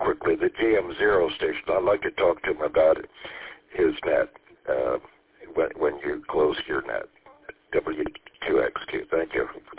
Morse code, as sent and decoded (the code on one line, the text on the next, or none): none